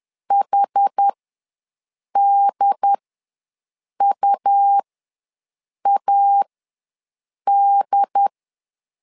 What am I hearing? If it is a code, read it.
Morse code, heard as a keyed tone, about 10 wpm, spaced as for 6 wpm, 800 Hz. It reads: HDUAD
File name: LEZIONI / 2 UHDN / 2 UHD ws 10wpm 2058